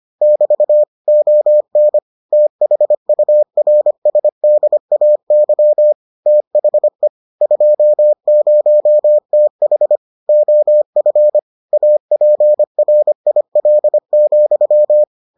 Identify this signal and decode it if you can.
Morse code, heard as a keyed tone, about 25 wpm, 600 Hz. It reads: <BT> ON THURSDAY THE 20TH OF APRIL,